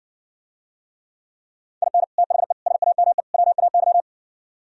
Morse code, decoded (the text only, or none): IN THE SAME WAY